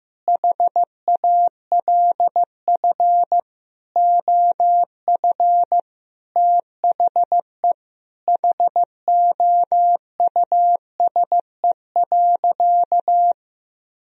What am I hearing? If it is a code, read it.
Morse code, heard as a keyed tone, about 15 wpm, 700 Hz. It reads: HALF OF THE HOUSE.
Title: Sentences from Top 400 Words and Abbreviations 15wpm